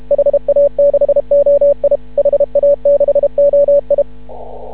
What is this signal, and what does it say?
Morse code, heard as a keyed tone, about 31 wpm, 560 Hz. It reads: HA6OI HA6OI
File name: CW / ha6oi 2